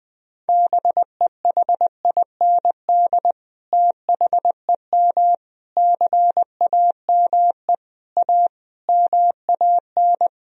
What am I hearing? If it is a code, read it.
Morse code, heard as a keyed tone, about 20 wpm, 700 Hz. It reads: BEHIND THEM CAME A MAN
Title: Sentences from Top 1000 Words - Review Part 4 - 20wpm